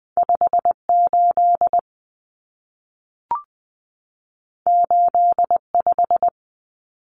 Morse code, heard as a keyed tone, about 20 wpm, 700 Hz. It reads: 58 85
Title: Common Cut Numbers - 2-Digit 20wpm